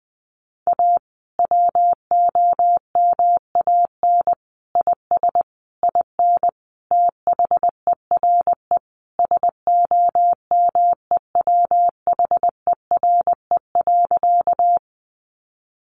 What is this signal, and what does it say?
Morse code, heard as a keyed tone, about 20 wpm, 700 Hz. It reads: A WOMAN IS IN THERE SOMEWHERE.
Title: Sentences from Top 1400 Words 20wpm